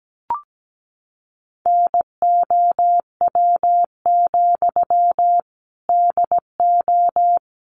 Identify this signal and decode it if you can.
Morse code, heard as a keyed tone, about 17 wpm, 700 Hz. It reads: NOW, DO
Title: Sentences from Top 700 Words 17wpm